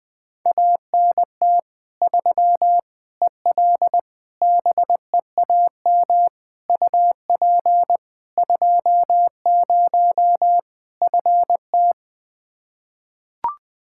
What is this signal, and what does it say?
Morse code, heard as a keyed tone, about 21 wpm, 700 Hz. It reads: ANT 3 EL BEAM UP 20 FT
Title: Practice QSO - Exchange 2 - Info - 20wpm